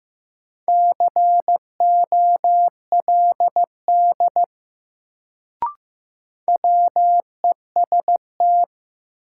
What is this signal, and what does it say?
Morse code, heard as a keyed tone, about 15 wpm, 700 Hz. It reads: COLD WEST
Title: Sentences from Top 400 Words - New Word Drill - 15wpm